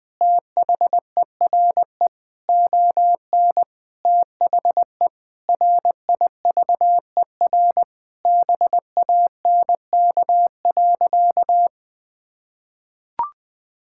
Morse code, heard as a keyed tone, about 21 wpm, 700 Hz. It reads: THERE ON THE RIVER BANK.
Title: Sentences from Top 1000 Words - Review Part 6 - 20wpm